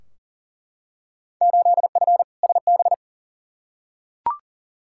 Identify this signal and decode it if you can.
Morse code, heard as a keyed tone, about 40 wpm, 700 Hz. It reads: 8F SB
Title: Field Day Exchanges Encore - 40wpm